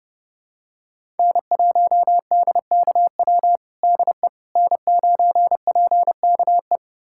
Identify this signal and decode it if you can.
Morse code, heard as a keyed tone, about 31 wpm, 700 Hz. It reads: N1DKW DE N9PKE
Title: USA Callsign DE Callsign - 5 Characters - 30wpm